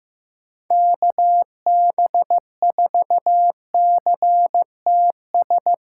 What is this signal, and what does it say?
Morse code, heard as a keyed tone, about 15 wpm, 700 Hz. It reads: KB4CTS
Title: USA Callsign DE Callsign 15wpm